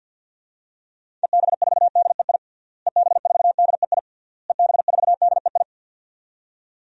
Morse code, heard as a keyed tone, about 52 wpm, 700 Hz. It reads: EB4DEI EB4DEI EB4DEI
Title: DXCC-100 Call Signs - 6 Characters - Repeated 3x - 50wpm